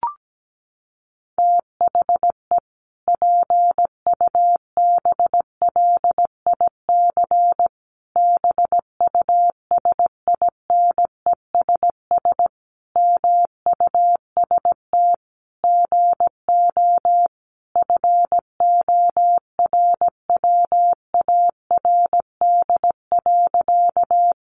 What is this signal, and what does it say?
Morse code, heard as a keyed tone, about 17 wpm, 700 Hz. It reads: THE PUBLIC BUSINESS MUST GO FORWARD.
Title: Sentences from Top 1300 Words 17wpm